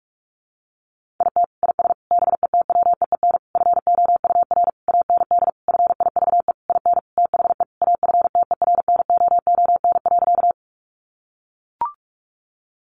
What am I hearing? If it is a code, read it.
Morse code, heard as a keyed tone, about 44 wpm, 700 Hz. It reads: IT IS BETWEEN FOUR AND FIVE IN THE AFTERNOON.
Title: Sentences from Top 1500 Words 45wpm